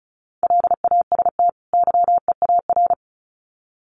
Morse code, heard as a keyed tone, about 34 wpm, 700 Hz. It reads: LAST YEAR
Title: Sets of 2 Words 35wpm